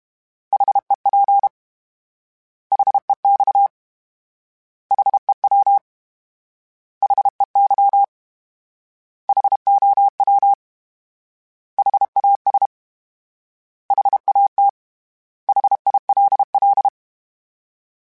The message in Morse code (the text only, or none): HEP HEX HEW HEY HOW HAS HAT HILL